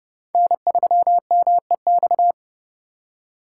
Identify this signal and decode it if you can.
Morse code, heard as a keyed tone, about 30 wpm, 700 Hz. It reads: N3MEX
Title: USA Call Signs - 5 Characters - 3x Character Spacing - 30wpm